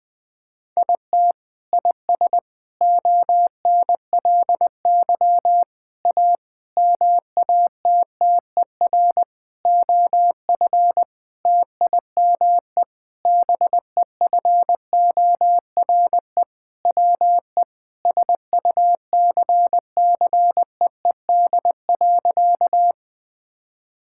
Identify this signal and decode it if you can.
Morse code, heard as a keyed tone, about 20 wpm, 700 Hz. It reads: IT IS ONLY A MATTER OF TIME BEFORE WE SUCCEED.
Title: Sentences with 6-Word Phrases - 20wpm